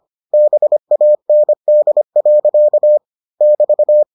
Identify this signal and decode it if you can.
Morse code, heard as a keyed tone, about 25 wpm, 600 Hz. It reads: BAND. <BT>